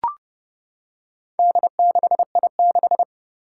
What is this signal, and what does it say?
Morse code, heard as a keyed tone, about 29 wpm, 700 Hz. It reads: D6I6
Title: Four Challenging Letter-Number Combinations 30wpm